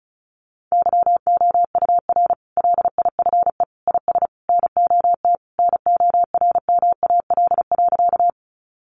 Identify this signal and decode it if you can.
Morse code, heard as a keyed tone, about 34 wpm, 700 Hz. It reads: YOUR LIFE IS NOT NORMAL.
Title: Sentences from Top 1200 Words 35wpm